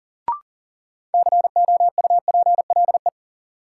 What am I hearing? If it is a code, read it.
Morse code, heard as a keyed tone, about 40 wpm, 700 Hz. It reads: COUPLE
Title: Sets of 2 Words - New Word Drill - 40wpm